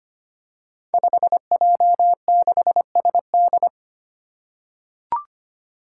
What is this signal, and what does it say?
Morse code, heard as a keyed tone, about 25 wpm, 700 Hz. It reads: 5J6SD
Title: Five Challenging Letter-Number Combinations 25wpm